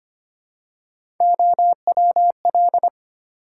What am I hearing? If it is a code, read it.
Morse code, heard as a keyed tone, about 25 wpm, 700 Hz. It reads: OWL
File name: practice / 25wpm / Easy Words with Hints - 25wpm